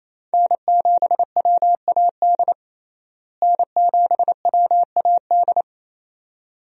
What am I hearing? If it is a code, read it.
Morse code, heard as a keyed tone, about 28 wpm, 700 Hz. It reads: N7WAD N7WAD
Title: USA Call Signs - 5 Characters - Repeated 2x - 28wpm